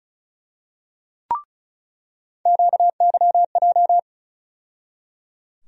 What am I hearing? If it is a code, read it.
Morse code, heard as a keyed tone, about 36 wpm, 700 Hz. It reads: QYJ
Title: USA Call Sign Postfixes - 3 Letters 35wpm